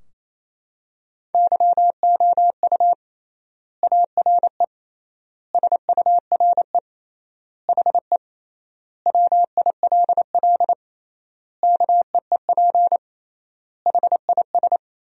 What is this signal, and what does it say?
Morse code, heard as a keyed tone, about 28 wpm, 700 Hz. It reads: YOU ARE SURE HE WILL KEEP HIS
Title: 3x Word-Spacing - Sentences from Top 300 Words - 28wpm